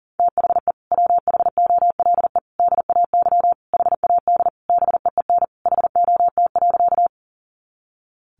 Morse code, heard as a keyed tone, about 40 wpm, 700 Hz. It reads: THE WHOLE DAY HAD BEEN HOT.
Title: Sentences from Top 500 Words 40wpm